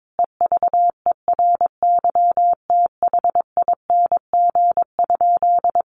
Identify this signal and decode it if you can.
Morse code, heard as a keyed tone, about 22 wpm, 700 Hz. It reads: EVERYTHING?
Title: Sentences from Top 1100 Words 22wpm